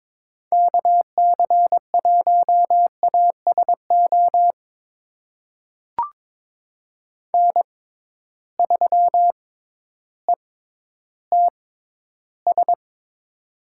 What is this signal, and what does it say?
Morse code, heard as a keyed tone, about 22 wpm, spaced as for 12 wpm, 700 Hz. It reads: KC1ASO N3ETS
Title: USA Call Signs - 6x Character Spacing - 22wpm